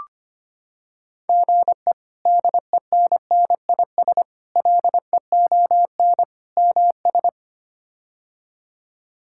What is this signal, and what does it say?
Morse code, heard as a keyed tone, about 25 wpm, 700 Hz. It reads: GE DENNIS LEON MS
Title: K1USN SST QSO Practice - 25wpm